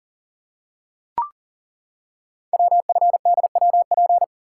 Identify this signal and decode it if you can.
Morse code, heard as a keyed tone, about 40 wpm, 690 Hz. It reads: WFDWP
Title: Code Groups - 5 Letters - 40wpm